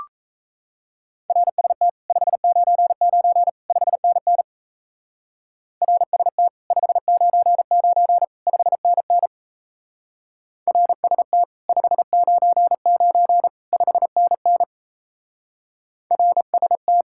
Morse code, heard as a keyed tone, about 34 wpm, 700 Hz. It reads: RST 599 5NN RST 599 5NN RST 599 5NN RST
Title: Speed-Racing Common QSO Phrases - 28wpm